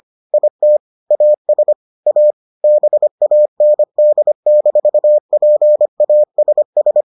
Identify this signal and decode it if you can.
Morse code, heard as a keyed tone, about 25 wpm, 600 Hz. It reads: IT AS A BAND-PASS